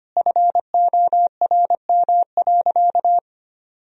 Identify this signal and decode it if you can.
Morse code, heard as a keyed tone, about 25 wpm, 700 Hz. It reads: FORM.
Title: Sentences from Top 1000 Words - Review Part 1 - 25wpm